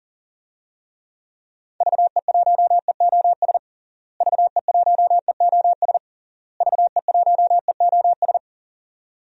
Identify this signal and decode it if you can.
Morse code, heard as a keyed tone, about 37 wpm, 700 Hz. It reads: VE1EOS VE1EOS VE1EOS
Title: DXCC-100 Call Signs - 6 Characters - Repeated 3x - 40wpm